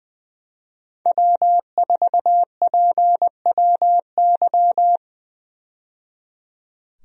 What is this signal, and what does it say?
Morse code, heard as a keyed tone, about 20 wpm, 700 Hz. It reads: W4PWY
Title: USA Call Signs - 5 Characters - 20wpm